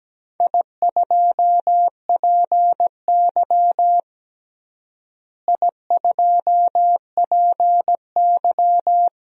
Morse code, heard as a keyed tone, about 17 wpm, 700 Hz. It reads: I2PY I2PY
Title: DXCC-100 Call Signs - 4 Characters - Repeated 3x - 17wpm